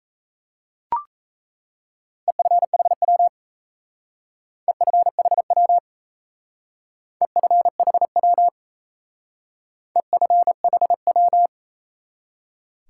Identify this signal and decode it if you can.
Morse code, heard as a keyed tone, about 34 wpm, 700 Hz. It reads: EFHW EFHW EFHW EFHW
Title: Speed-Racing Top 180 QSO Elements 28wpm